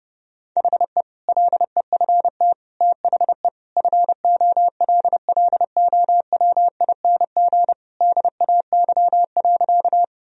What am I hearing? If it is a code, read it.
Morse code, heard as a keyed tone, about 32 wpm, 700 Hz. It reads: HE LEFT THE FOLLOWING DAY.
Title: Sentences from Top 1000 Words - Review Part 4 - 30wpm